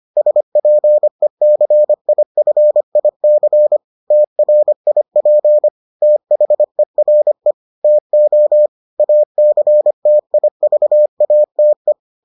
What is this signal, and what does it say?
Morse code, heard as a keyed tone, about 25 wpm, 600 Hz. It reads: SPECIFIC TRIP THERE TO ACTIVATE